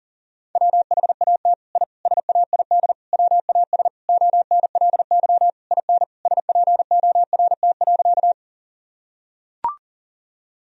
Morse code, heard as a keyed tone, about 40 wpm, 700 Hz. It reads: WHAT I SAID WAS ONLY IN SPORT.
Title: Sentences from Top 1000 Words - Review Part 4 - 40wpm